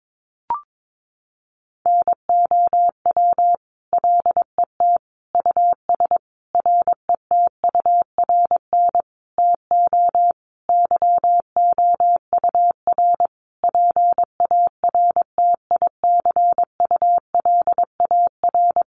Morse code, heard as a keyed tone, about 22 wpm, 700 Hz. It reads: NOW LET US RETURN TO YOUR PARTICULAR
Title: Sentences from Top 800 Words 22wpm